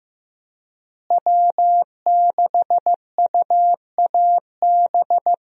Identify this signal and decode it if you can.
Morse code, heard as a keyed tone, about 15 wpm, 700 Hz. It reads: W6UAB
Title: USA Call Signs - 5 Characters - Repeated 2x - 15wpm